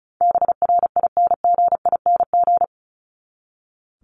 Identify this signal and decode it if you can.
Morse code, heard as a keyed tone, about 34 wpm, 700 Hz. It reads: BRINGING
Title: ING Suffix - Top 100 Words with Base Words 35wpm